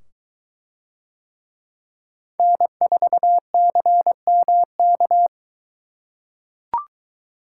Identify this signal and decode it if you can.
Morse code, heard as a keyed tone, about 22 wpm, 700 Hz. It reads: N4CMK